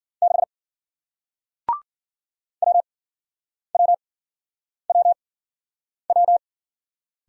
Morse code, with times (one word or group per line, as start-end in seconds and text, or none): 0.2-0.4 6
2.6-6.4 WWWW